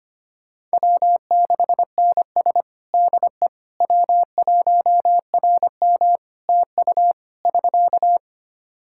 Morse code, heard as a keyed tone, about 25 wpm, 700 Hz. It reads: W6NS DE W1RM TU <SK>
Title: Practice QSO - Exchange 4 - Ending - 25wpm